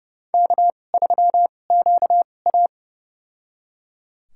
Morse code, heard as a keyed tone, about 32 wpm, spaced as for 22 wpm, 700 Hz. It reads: K3QA
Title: USA Call Signs - 4 Characters - 2x Character Spacing - 30wpm